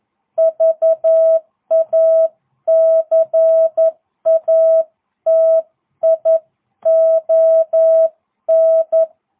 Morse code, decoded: VACATION

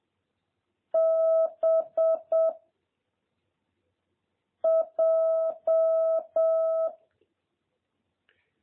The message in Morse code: B J

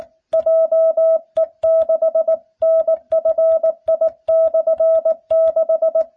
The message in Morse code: JE6NFI/6

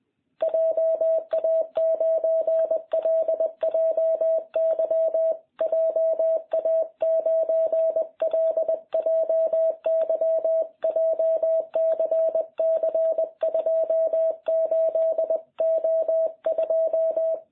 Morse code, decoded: JA9LJY JA9LJY JCC 28O2

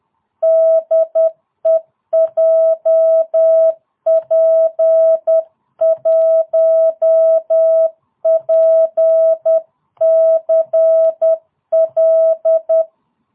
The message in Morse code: DE JP1PCL

